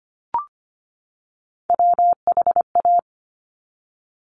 Morse code, heard as a keyed tone, about 25 wpm, 700 Hz. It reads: WHA
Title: Top Three Letter Combinations in Words 25wpm